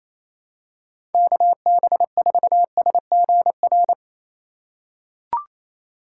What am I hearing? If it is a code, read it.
Morse code, heard as a keyed tone, about 27 wpm, 700 Hz. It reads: KB4SGR
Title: USA Call Signs - 6 Characters - 2x Character Spacing - 28wpm